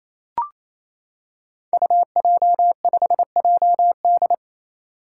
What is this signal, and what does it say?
Morse code, heard as a keyed tone, about 29 wpm, 700 Hz. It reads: UJ5JD
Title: Code Groups - 5 Letter:Numbers - 28wpm